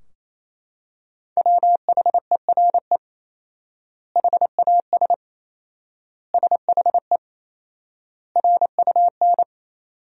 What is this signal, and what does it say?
Morse code, heard as a keyed tone, about 28 wpm, 700 Hz. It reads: WHERE HAS SHE RUN
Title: 4x Word-Spacing - Sentences from Top 300 Words - 28wpm